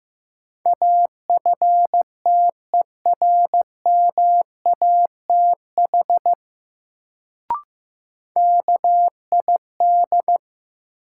Morse code, heard as a keyed tone, about 15 wpm, 700 Hz. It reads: AFTERMATH KID
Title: Sets of 5 Words - New Word Drill - 15wpm